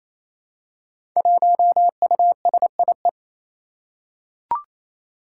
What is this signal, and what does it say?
Morse code, heard as a keyed tone, about 28 wpm, 700 Hz. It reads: 1USIE
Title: Five Challenging Letter-Number Combinations 28wpm